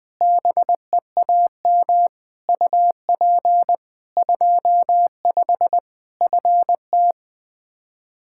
Morse code, heard as a keyed tone, about 20 wpm, 700 Hz. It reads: BEAM UP 25 FT
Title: K1USN SST QSO Practice 20wpm